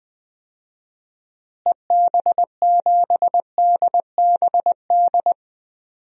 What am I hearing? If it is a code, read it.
Morse code, heard as a keyed tone, about 20 wpm, 700 Hz. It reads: EB7DBD
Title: DXCC-100 Call Signs - 6 Characters - Repeated 3x - 20wpm